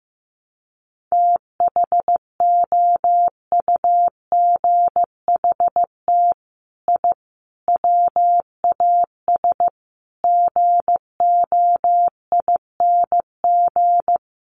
Morse code, heard as a keyed tone, about 15 wpm, 700 Hz. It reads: THOUGHT I WAS GOING
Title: Sets of 5 Words Encore2 15wpm